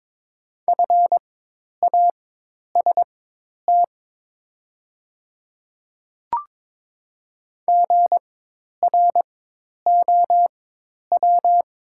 Morse code, heard as a keyed tone, about 22 wpm, spaced as for 10 wpm, 700 Hz. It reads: FAST GROW
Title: 4x Character Spacing - Top 500 Words - 22wpm